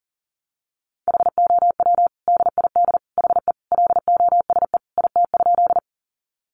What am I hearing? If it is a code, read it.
Morse code, heard as a keyed tone, about 40 wpm, 700 Hz. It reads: HOW DID HE LOSE IT?
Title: Sentences from Top 1100 Words 40wpm